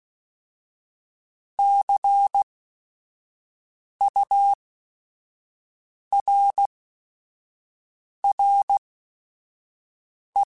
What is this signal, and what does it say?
Morse code, heard as a keyed tone, about 16 wpm, spaced as for 5 wpm, 800 Hz. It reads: CURRE
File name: Technical Subjects / Technical Subjects-mp3s / XL @5